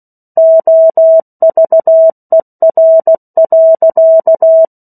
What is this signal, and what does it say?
Morse code, heard as a keyed tone, about 16 wpm, 650 Hz. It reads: OVER.
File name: 03-Chapter-03 16 WPM